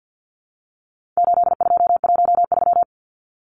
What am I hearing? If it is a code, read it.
Morse code, heard as a keyed tone, about 50 wpm, 700 Hz. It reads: 8213
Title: Numbers - 4-Digit 50wpm